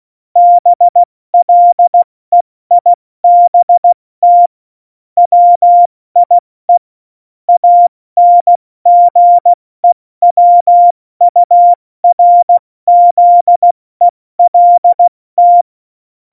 Morse code, heard as a keyed tone, about 16 wpm, 700 Hz. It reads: BLEIBT WIE ANGEWURZELT
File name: Morsekurs / Witze / Witz 26 Mai, 2020 70 F80